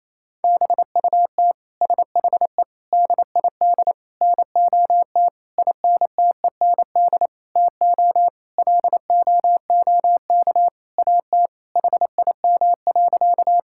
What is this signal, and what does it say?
Morse code, heard as a keyed tone, about 28 wpm, 700 Hz. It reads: BUT SHE DID NOT INTEND TO LOOK AT HIM.
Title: Sentences from Top 1600 Words 28wpm